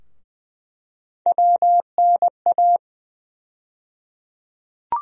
Morse code, heard as a keyed tone, about 20 wpm, 700 Hz. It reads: WNA
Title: USA Call Sign Postfixes - 3 Letters 20wpm